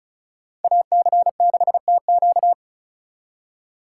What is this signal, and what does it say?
Morse code, heard as a keyed tone, about 46 wpm, 700 Hz. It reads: AC6TQ